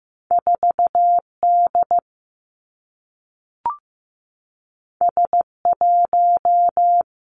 Morse code, heard as a keyed tone, about 15 wpm, 700 Hz. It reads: 4 D S 1
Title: Two Challenging Letter-Number Combinations 15wpm